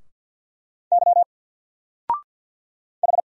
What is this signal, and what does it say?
Morse code, heard as a keyed tone, about 48 wpm, 700 Hz. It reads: Y H
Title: Single Character (Letter:Number:Punctuation:Prosign) 50wpm